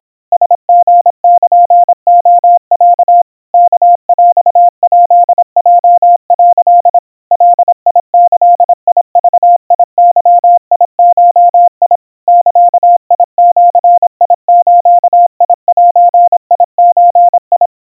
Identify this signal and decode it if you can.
Morse code (Japanese Wabun code), heard as a keyed tone, about 26 wpm, 700 Hz. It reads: ラリルレロ ワヰヱヲ」 ガギグゲゴ ザジズゼゾ
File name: aiueo 90 cpm 700 Hz Crispy